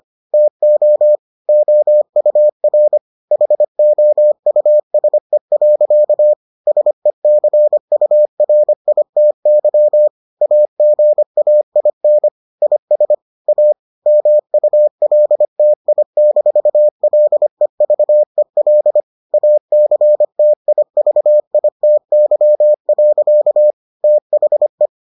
Morse code, heard as a keyed tone, about 25 wpm, 600 Hz. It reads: TO OUR HOUSE. SECURITY AGAIN IS A MULTI-LEVEL ACTIVITY. THE